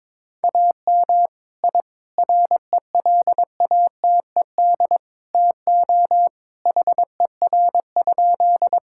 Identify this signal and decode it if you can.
Morse code, heard as a keyed tone, about 22 wpm, 700 Hz. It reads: AM I RELATED TO HER?
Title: Sentences from Top 1000 Words - Review Part 5 - 22wpm